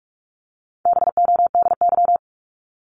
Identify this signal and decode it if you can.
Morse code, heard as a keyed tone, about 44 wpm, 700 Hz. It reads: BODY
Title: Top 500 Words 45wpm